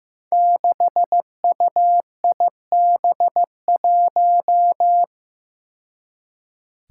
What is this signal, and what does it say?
Morse code, heard as a keyed tone, about 15 wpm, 700 Hz. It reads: 6UIB1
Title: Five Challenging Letter-Number Combinations 15wpm